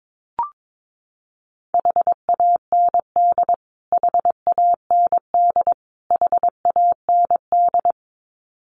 Morse code, heard as a keyed tone, about 22 wpm, 700 Hz. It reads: HAND HAND HAND
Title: Top 200 Words Repeated 22wpm